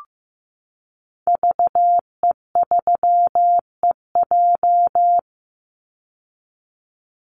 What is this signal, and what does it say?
Morse code, heard as a keyed tone, about 15 wpm, 700 Hz. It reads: VE3EJ
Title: Call Signs Repeated 3x - 15wpm